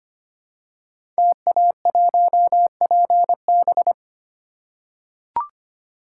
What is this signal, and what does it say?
Morse code, heard as a keyed tone, about 25 wpm, 700 Hz. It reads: TA1PB
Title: Code Groups - 5 Letter:Numbers - 25wpm